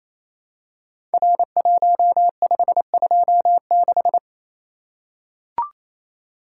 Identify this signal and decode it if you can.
Morse code, heard as a keyed tone, about 29 wpm, 700 Hz. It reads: R1526
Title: Code Groups - 5 Letter:Numbers - 28wpm